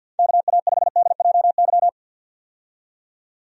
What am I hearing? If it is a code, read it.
Morse code, heard as a keyed tone, about 53 wpm, 700 Hz. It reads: KA5DJY
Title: USA Call Signs - 6 Characters - 50wpm